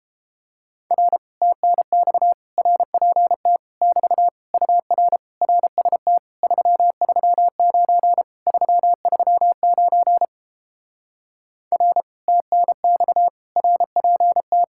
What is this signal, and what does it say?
Morse code, heard as a keyed tone, about 29 wpm, 700 Hz. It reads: R TNX RPT <BT> UR RST 339 339 R TNX RPT